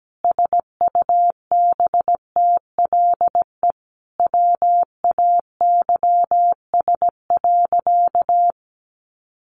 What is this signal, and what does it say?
Morse code, heard as a keyed tone, about 17 wpm, 700 Hz. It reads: SUBTLE WAYS.
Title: A Sound Of Thunder 17wpm